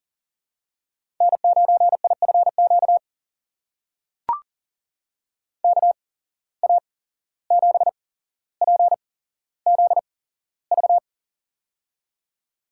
Morse code, heard as a keyed tone, about 39 wpm, spaced as for 16 wpm, 700 Hz. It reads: N9IFQ KA7PZV
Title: USA Call Signs - 8x Character Spacing - 40wpm